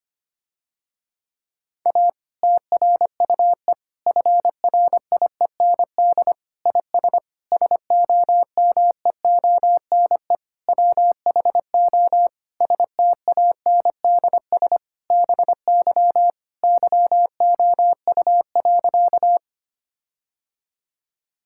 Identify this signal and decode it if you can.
Morse code, heard as a keyed tone, about 25 wpm, 700 Hz. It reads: A TRUE FRIEND IS SOMEONE WHO STANDS BY YOU.
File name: practice / 25wpm / Difficult Sentences with Hints - 25wpm